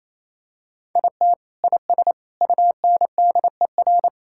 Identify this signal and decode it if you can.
Morse code, heard as a keyed tone, about 28 wpm, 700 Hz. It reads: IT IS UNDER
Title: Sentences from Top 1000 Words - Review Part 6 - 28wpm